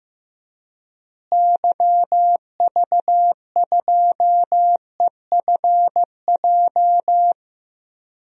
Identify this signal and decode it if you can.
Morse code, heard as a keyed tone, about 16 wpm, 700 Hz. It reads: YV2EFJ
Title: DXCC-100 Call Signs - 6 Characters - Repeated 2x - 15wpm